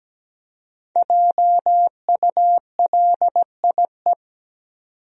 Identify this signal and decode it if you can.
Morse code, heard as a keyed tone, about 17 wpm, 700 Hz. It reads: JULIE
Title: USA Common Names - Max 5 Letters - 17wpm